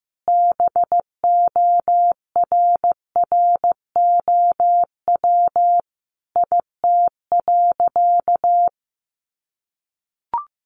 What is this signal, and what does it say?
Morse code, heard as a keyed tone, about 15 wpm, 700 Hz. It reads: BORROW IT.
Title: Sentences from Top 1500 Words 15wpm